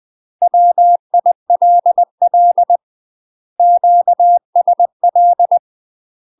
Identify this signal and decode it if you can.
Morse code, heard as a keyed tone, about 20 wpm, 700 Hz. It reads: WILL QSL
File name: UK MASTER QSOs TRAINING / 20WPM / UK MASTER QSOs 0046